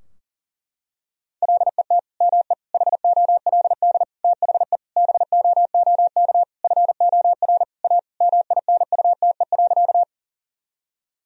Morse code, heard as a keyed tone, about 40 wpm, 700 Hz. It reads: LET ME HOLD THE BOOK FOR A MINUTE.
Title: Sentences from Top 1000 Words - Review Part 2 - 40wpm